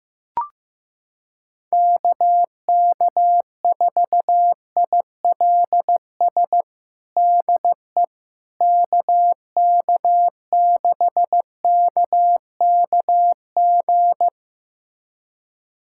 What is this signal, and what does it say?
Morse code, heard as a keyed tone, about 15 wpm, 700 Hz. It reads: KK4ILS DE KK6KKG
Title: USA Callsign DE Callsign - 6 Characters - 15wpm